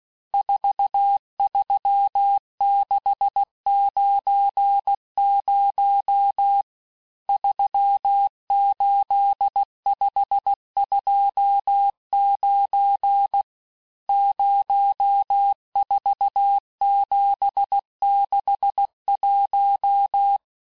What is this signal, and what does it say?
Morse code, heard as a keyed tone, about 16 wpm, 800 Hz. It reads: 43690 38529 04761
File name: cours-du-lundi files / cours-1592--teste-chiffre-vitesse-15mpm